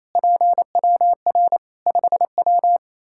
Morse code, heard as a keyed tone, about 28 wpm, 700 Hz. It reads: PWR 5W
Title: Practice QSO - Exchange 2 - Info - 28wpm